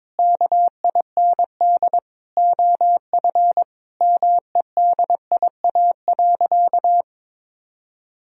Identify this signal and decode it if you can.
Morse code, heard as a keyed tone, about 21 wpm, 700 Hz. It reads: KIND OF MEDIA.
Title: Sentences from Top 1000 Words - Review Part 4 - 22wpm